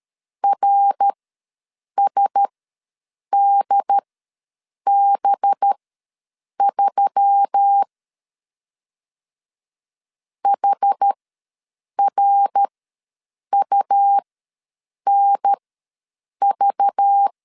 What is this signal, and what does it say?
Morse code, heard as a keyed tone, about 13 wpm, spaced as for 7 wpm, 800 Hz. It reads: RSDB3 HRUNV
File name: LEZIONI / 5 LBVC / 5 LBV ws 12wpm 3015